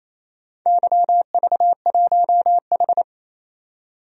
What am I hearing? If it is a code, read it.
Morse code, heard as a keyed tone, about 28 wpm, 700 Hz. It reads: YV1H
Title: DXCC-100 Call Signs - 4 Characters - 28wpm